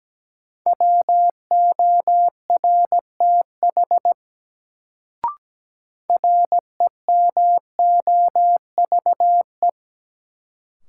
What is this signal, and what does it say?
Morse code, heard as a keyed tone, about 17 wpm, 700 Hz. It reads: WORTH REMOVE
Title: Sentences from Top 800 Words - New Word Drill - 17wpm